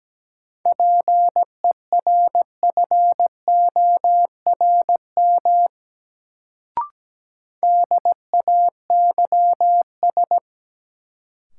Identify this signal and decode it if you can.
Morse code, heard as a keyed tone, about 17 wpm, 700 Hz. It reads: PERFORM DAYS
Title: Sentences from Top 1000 Words - Review Part 5 - New Word Drill - 17wpm